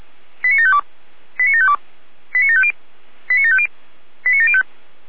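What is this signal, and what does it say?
Five-tone selective call (ZVEI), heard as quick runs of five tones, tones of about 70 ms. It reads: 89752 89752 89750 89750 89705
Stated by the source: ZVEI five-tone selective call received on an SDR.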